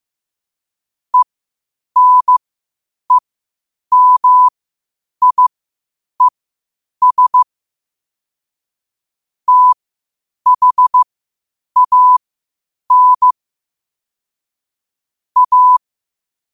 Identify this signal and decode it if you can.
Morse code, heard as a keyed tone, about 15 wpm, spaced as for 8 wpm, 1000 Hz. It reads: ENEMIES THAN A
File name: qotdfiles / 8wpm